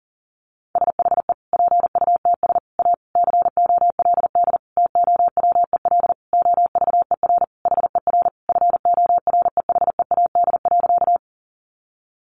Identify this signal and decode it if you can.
Morse code, heard as a keyed tone, about 40 wpm, 700 Hz. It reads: SHE PUTS A COLD TOWEL OVER HER FOREHEAD.
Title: Easy Phrases 40wpm